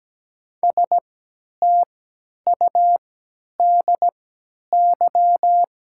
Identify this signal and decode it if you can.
Morse code, heard as a keyed tone, about 17 wpm, spaced as for 10 wpm, 690 Hz. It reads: STUDY
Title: 3x Character Spacing - Top 300 Words - 17wpm